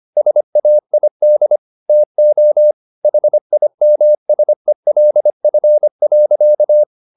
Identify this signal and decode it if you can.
Morse code, heard as a keyed tone, about 25 wpm, 600 Hz. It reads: SAID TO HIMSELF.
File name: the old man and the sea part 1 25wpm